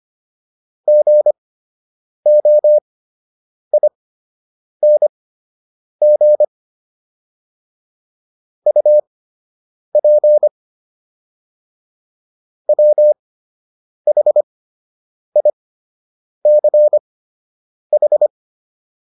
Morse code, heard as a keyed tone, about 25 wpm, spaced as for 8 wpm, 600 Hz. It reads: GOING UP WHICH